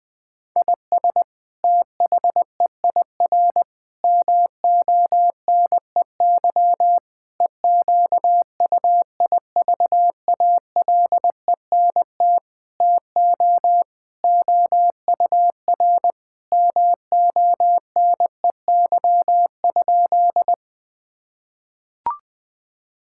Morse code, heard as a keyed tone, about 20 wpm, 700 Hz. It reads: IS THEIR MONEY EQUIVALENT TO OUR MONEY?
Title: Sentences from Top 1600 Words 20wpm